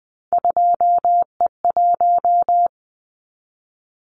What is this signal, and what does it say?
Morse code, heard as a keyed tone, about 20 wpm, 700 Hz. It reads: 2E1
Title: Callsign Prefixes for Top 25 DXCC - 20wpm